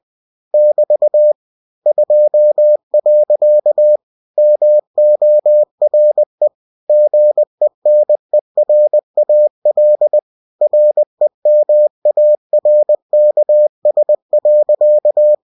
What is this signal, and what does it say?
Morse code, heard as a keyed tone, about 20 wpm, 600 Hz. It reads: <BT> 2. MORE GENERAL REMARKS.
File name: cw-books / winnie-the-pooh / winnie-20wpm-page-0063